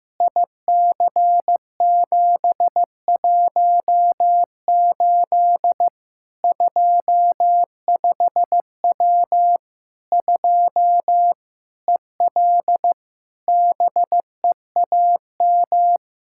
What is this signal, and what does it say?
Morse code, heard as a keyed tone, about 15 wpm, 700 Hz. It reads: IC718 25W 2 EL BEAM
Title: Practice QSO - Exchange 2 - Info - 15wpm